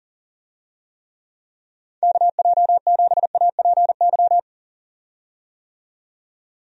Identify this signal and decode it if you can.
Morse code, heard as a keyed tone, about 40 wpm, 700 Hz. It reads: KJ7APY
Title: USA Call Signs - 6 Characters - 40wpm